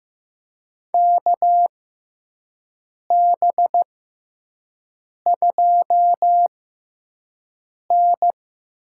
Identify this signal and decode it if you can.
Morse code, heard as a keyed tone, about 15 wpm, spaced as for 5 wpm, 700 Hz. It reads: KB2N
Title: USA Call Signs - 4 Characters - 6x Character Spacing - 15wpm